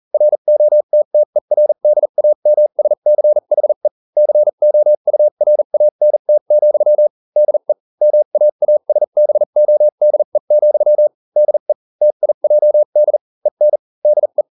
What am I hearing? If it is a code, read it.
Morse code, heard as a keyed tone, about 40 wpm, spaced as for 35 wpm, 600 Hz. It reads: ROTTERDAMSCHE COURANT, DE MAASBODE, DE TIJD EN DE